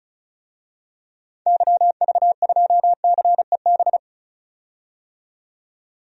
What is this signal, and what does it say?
Morse code, heard as a keyed tone, about 34 wpm, 700 Hz. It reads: YV2CEB
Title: DXCC-100 Call Signs - 6 Characters - 35wpm